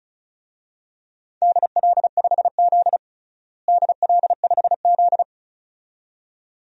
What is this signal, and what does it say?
Morse code, heard as a keyed tone, about 36 wpm, 700 Hz. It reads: DL5Z DL5Z
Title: DXCC-100 Call Signs - 4 Characters - Repeated 2x - 35wpm